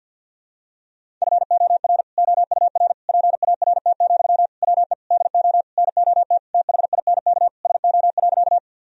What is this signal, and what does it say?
Morse code, heard as a keyed tone, about 46 wpm, 700 Hz. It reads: FOR OUR PART, WE DO NOT THINK SO.
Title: Sentences from Top 1000 Words - Review Part 1 - 50wpm